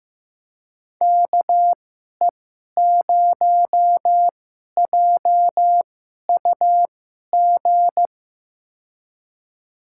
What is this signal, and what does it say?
Morse code, heard as a keyed tone, about 15 wpm, spaced as for 11 wpm, 700 Hz. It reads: KE0JUG
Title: USA Call Signs - 6 Characters - 2x Character Spacing - 15wpm